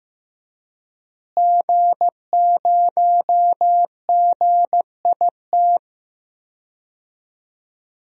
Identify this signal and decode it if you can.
Morse code, heard as a keyed tone, about 15 wpm, 700 Hz. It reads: G0GIT
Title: DXCC-100 Call Signs - 5 Characters - Repeated 2x - 15wpm